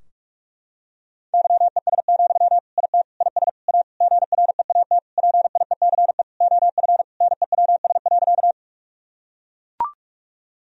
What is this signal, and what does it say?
Morse code, heard as a keyed tone, about 44 wpm, 700 Hz. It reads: YES, IT IS A GREAT PIECE OF NEWS.
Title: Sentences from Top 1000 Words - Review Part 2 - 45wpm